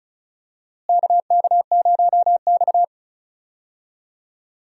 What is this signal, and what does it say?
Morse code, heard as a keyed tone, about 36 wpm, 700 Hz. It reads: KK0X
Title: USA Call Signs - 4 Characters - Repeated 3x - 35wpm